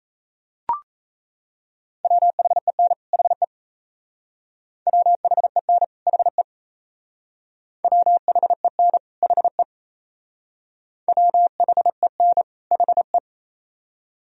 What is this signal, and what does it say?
Morse code, heard as a keyed tone, about 35 wpm, 700 Hz. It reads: WHEN HE WHEN HE WHEN HE WHEN HE
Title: Speed-Racing - Sets of 2 Words 28wpm